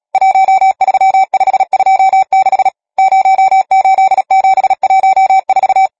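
Morse code, heard as a keyed tone, about 37 wpm, 770 Hz. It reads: 13526 08714
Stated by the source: Morse code number practice, machine-sent training text (Nácvik telegrafie, OK1FCB).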